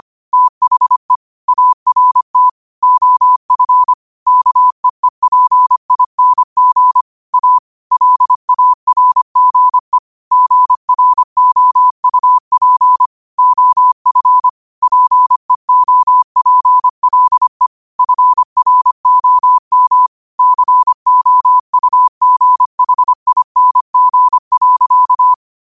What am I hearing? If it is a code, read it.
Morse code, heard as a keyed tone, about 25 wpm, 1000 Hz. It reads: THE ART OF KEEPING A LARGE GROUP OF PEOPLE FROM COUGHING.
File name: qotdfiles / 25wpm